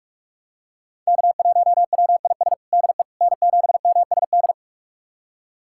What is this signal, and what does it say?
Morse code, heard as a keyed tone, about 41 wpm, 700 Hz. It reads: K1WIS DE N7MSD
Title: USA Callsign DE Callsign 45wpm